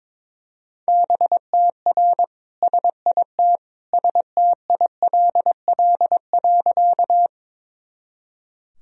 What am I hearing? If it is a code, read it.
Morse code, heard as a keyed tone, about 22 wpm, 700 Hz. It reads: BTR SIT STILL.
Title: Sentences from Top 500 Words and Abbreviations 22wpm